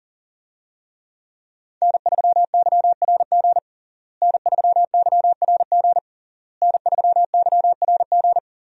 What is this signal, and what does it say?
Morse code, heard as a keyed tone, about 40 wpm, 700 Hz. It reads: N3YRG N3YRG N3YRG
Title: USA Call Signs - 5 Characters - Repeated 3x - 40wpm